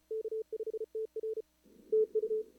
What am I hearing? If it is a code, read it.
Morse code, heard as a keyed tone, about 36 wpm, 440 Hz. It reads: K5TR TU
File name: CWSprint / Feb2022 / k5tr-feb-na-sprint-cw-k5ot -0100